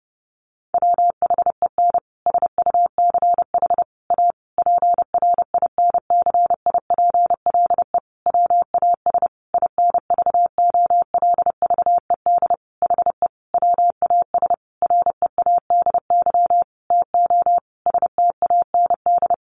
Morse code, heard as a keyed tone, about 30 wpm, 700 Hz. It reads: WHEN SUCH A PRINCIPLE WAS INVOLVED HE WAS READY TO STAND
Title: Sentences from Top 1400 Words 30wpm